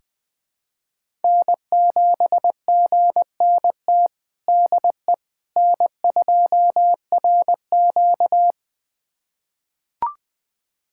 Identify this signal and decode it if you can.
Morse code, heard as a keyed tone, about 21 wpm, 700 Hz. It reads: N7GNT DE N2RQ
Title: USA Callsign DE Callsign 20wpm